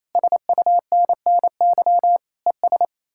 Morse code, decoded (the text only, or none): SUNNY ES